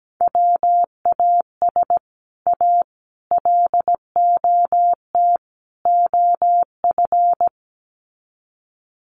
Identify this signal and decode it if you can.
Morse code, heard as a keyed tone, about 17 wpm, 700 Hz. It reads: WAS A LOT OF
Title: Sets of 4 Words 17wpm